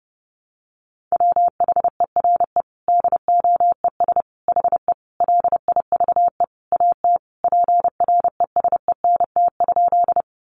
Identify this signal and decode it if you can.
Morse code, heard as a keyed tone, about 30 wpm, 700 Hz. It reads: WHERE DOES HE LIVE AT PRESENT?
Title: Sentences from Top 800 Words 30wpm